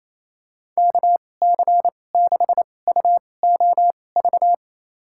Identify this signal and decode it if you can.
Morse code, heard as a keyed tone, about 28 wpm, spaced as for 20 wpm, 700 Hz. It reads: KC6UOV
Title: USA Call Signs - 6 Characters - 2x Character Spacing - 28wpm